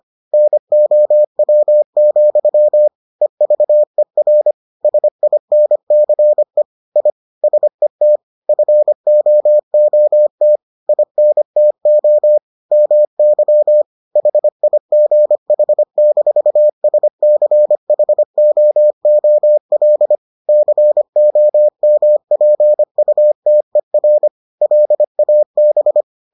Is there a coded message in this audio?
Yes: Morse code from 0.3 to 26.0 s, heard as a keyed tone, about 25 wpm, 600 Hz. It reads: NOW, EVER SINCE I SET FOOT INTO MY HIGH-SCHOOL COMPUTER LAB